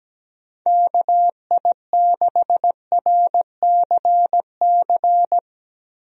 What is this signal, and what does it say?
Morse code, heard as a keyed tone, about 17 wpm, 700 Hz. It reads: KI6RCC
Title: USA Call Signs - 6 Characters - Repeated 2x - 17wpm